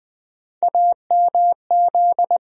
Morse code, heard as a keyed tone, about 20 wpm, 700 Hz. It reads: AMZ
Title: USA Call Sign Postfixes - 3 Letters 20wpm